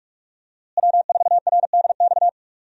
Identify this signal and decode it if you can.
Morse code, heard as a keyed tone, about 46 wpm, 700 Hz. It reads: W4RDX
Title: USA Call Signs - 5 Characters - Repeated 2x - 45wpm